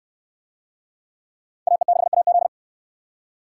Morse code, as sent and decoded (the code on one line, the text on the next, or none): .- . -.... .- --..
AE6AZ